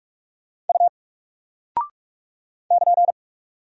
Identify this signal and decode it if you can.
Morse code, heard as a keyed tone, about 44 wpm, 700 Hz. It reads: U <KN>
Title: Single Character (Letter:Number:Punctuation:Prosign) 45wpm